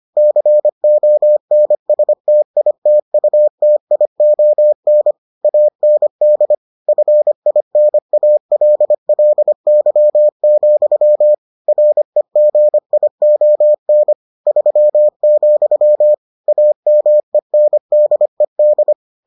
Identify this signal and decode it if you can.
Morse code, heard as a keyed tone, about 25 wpm, 600 Hz. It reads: CONSTITUTION AND FINALLY, REGION 3, AMENDED